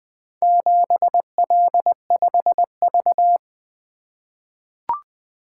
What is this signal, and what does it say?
Morse code, heard as a keyed tone, about 20 wpm, 700 Hz. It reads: 7L5V
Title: Four Letter-Number Combinations - 20wpm